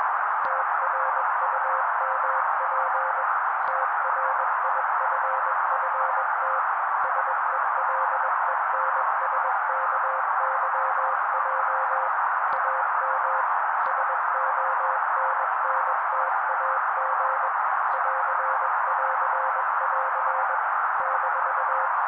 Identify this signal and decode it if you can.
Morse code, heard as a keyed tone, about 21 wpm, 530 Hz. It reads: TRUMP TRIFFT SELENSKYJ AM SONNTAG <AR><AR><AR> -
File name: daily / 20wpm noise